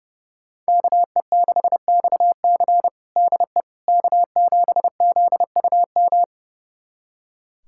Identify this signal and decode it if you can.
Morse code, heard as a keyed tone, about 30 wpm, 700 Hz. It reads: KE6XC DE K7ZUM
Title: USA Callsign DE Callsign - 5 Characters - 30wpm